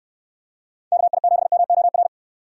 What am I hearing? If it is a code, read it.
Morse code, heard as a keyed tone, about 65 wpm, 700 Hz. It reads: KI7MOR